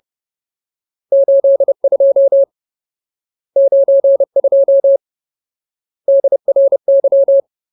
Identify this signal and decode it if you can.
Morse code, heard as a keyed tone, about 30 wpm, 550 Hz. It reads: 82 92 DRY
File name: Ultimate Word List / 30WPM Ultimate